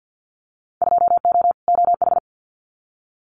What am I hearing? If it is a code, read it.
Morse code, heard as a keyed tone, about 50 wpm, 700 Hz. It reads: 2O OH
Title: Winter Field Day Exchanges 50wpm